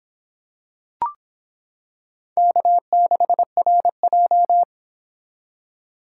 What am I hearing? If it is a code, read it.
Morse code, heard as a keyed tone, about 26 wpm, 700 Hz. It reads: K6RJ